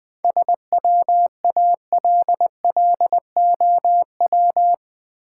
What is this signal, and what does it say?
Morse code, heard as a keyed tone, about 20 wpm, 700 Hz. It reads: SWALLOW
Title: 7 Letter Words - First 3 Letters Given - 20wpm